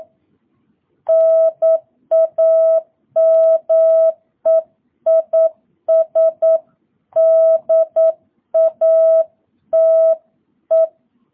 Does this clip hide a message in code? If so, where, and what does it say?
Yes: Morse code from 1.1 to 10.9 s, heard as a keyed tone, about 9 wpm, 660 Hz. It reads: NAME IS DATE